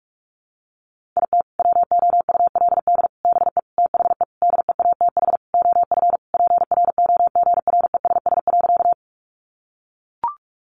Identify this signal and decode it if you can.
Morse code, heard as a keyed tone, about 44 wpm, 700 Hz. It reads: IT WOULD BE THE DEATH OF PROGRESS.
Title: Sentences from Top 1400 Words 45wpm